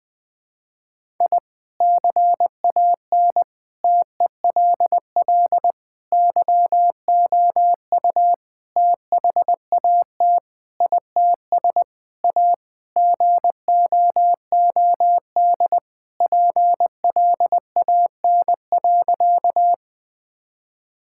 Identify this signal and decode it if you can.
Morse code, heard as a keyed tone, about 21 wpm, 700 Hz. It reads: I CAN TELL YOU THAT ITS A GOOD PLAN.
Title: Sentences with 5-Word Phrases - Spoken Beginning - 20wpm